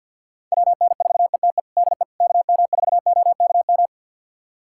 Morse code, heard as a keyed tone, about 51 wpm, 700 Hz. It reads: WN4ETE DE KM4OKM
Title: USA Callsign DE Callsign 50wpm